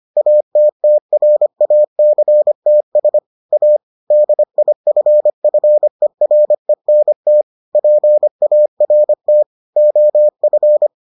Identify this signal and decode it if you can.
Morse code, heard as a keyed tone, about 25 wpm, 600 Hz. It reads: ATTRACTS A DIFFERENT PART OF